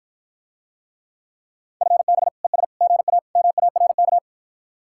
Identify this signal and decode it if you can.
Morse code, heard as a keyed tone, about 54 wpm, 700 Hz. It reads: FB ES GA MARK